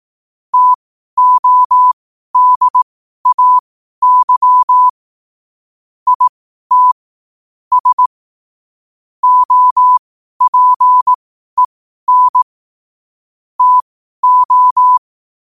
Morse code, heard as a keyed tone, about 18 wpm, spaced as for 12 wpm, 1000 Hz. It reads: TODAY IT S OPEN TO
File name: qotdfiles / 12wpm